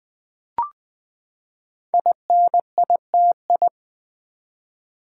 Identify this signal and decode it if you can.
Morse code, heard as a keyed tone, about 20 wpm, 700 Hz. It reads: INITI
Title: Build A Word - 5 Letters At A Time - 10-Letter Words - 20wpm